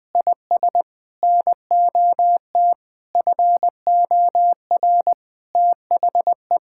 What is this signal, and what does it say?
Morse code, heard as a keyed tone, about 19 wpm, 700 Hz. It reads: IS NOT FOR THE
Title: Sentences with 6-Word Phrases - Spoken Beginning - 20wpm